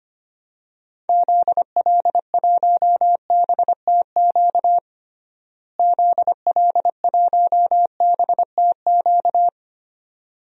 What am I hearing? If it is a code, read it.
Morse code, heard as a keyed tone, about 25 wpm, 700 Hz. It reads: ZL1BTQ ZL1BTQ
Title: DXCC-100 Call Signs - 6 Characters - Repeated 2x - 25wpm